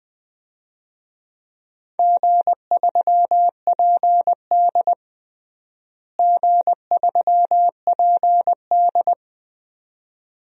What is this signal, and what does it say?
Morse code, heard as a keyed tone, about 20 wpm, 700 Hz. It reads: G3PD G3PD
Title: DXCC-100 Call Signs - 4 Characters - Repeated 2x - 20wpm